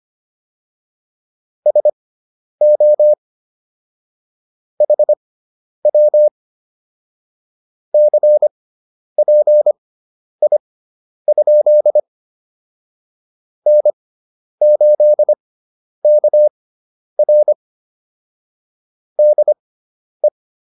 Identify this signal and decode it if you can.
Morse code, heard as a keyed tone, about 25 wpm, spaced as for 10 wpm, 600 Hz. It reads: SO HW CPI? N8KR DE
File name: QSO206 10